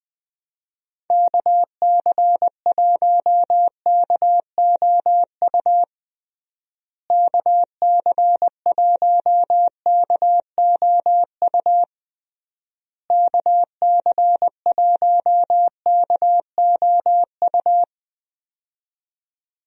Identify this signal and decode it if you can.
Morse code, heard as a keyed tone, about 20 wpm, 700 Hz. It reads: KC1KOU KC1KOU KC1KOU
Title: USA Call Signs - 6 Characters - Repeated 3x - 20wpm